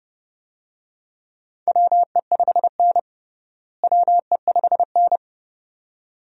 Morse code, heard as a keyed tone, about 32 wpm, 700 Hz. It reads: WE5N WE5N
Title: USA Call Signs - 4 Characters - Repeated 2x - 30wpm